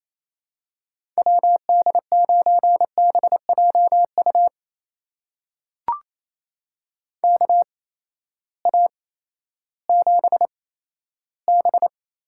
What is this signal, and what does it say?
Morse code, heard as a keyed tone, about 28 wpm, spaced as for 14 wpm, 700 Hz. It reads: WD9BJU KA7B
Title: USA Call Signs - 6 Characters - 8x Character Spacing - 28wpm